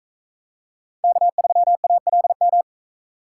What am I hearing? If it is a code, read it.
Morse code, heard as a keyed tone, about 41 wpm, 700 Hz. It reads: K3ALM